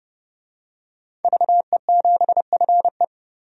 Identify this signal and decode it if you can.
Morse code, heard as a keyed tone, about 30 wpm, 700 Hz. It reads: VE7FE
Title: DXCC-100 Call Signs - 5 Characters - 30wpm